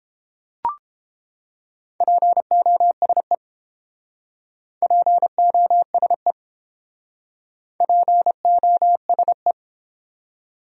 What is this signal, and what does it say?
Morse code, heard as a keyed tone, about 29 wpm, 700 Hz. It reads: POSE POSE POSE